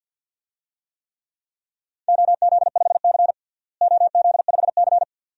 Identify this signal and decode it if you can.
Morse code, heard as a keyed tone, about 50 wpm, 700 Hz. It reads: OZ5C OZ5C
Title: DXCC-100 Call Signs - 4 Characters - Repeated 2x - 50wpm